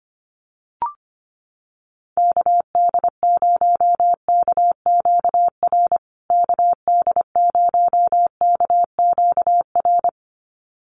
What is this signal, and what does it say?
Morse code, heard as a keyed tone, about 25 wpm, 700 Hz. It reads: KD0KQR KD0KQR
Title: USA Call Signs Repeated 2x 25wpm